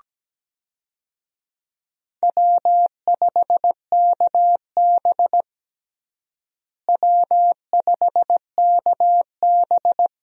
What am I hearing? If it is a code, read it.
Morse code, heard as a keyed tone, about 17 wpm, 700 Hz. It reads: W5KB W5KB
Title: USA Call Signs - 4 Characters - Repeated 2x - 17wpm